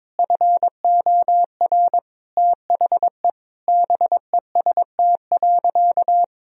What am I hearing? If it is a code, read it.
Morse code, heard as a keyed tone, about 22 wpm, 700 Hz. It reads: FOR THE BEST.
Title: Sentences from Top 1000 Words - Review Part 3 - 22wpm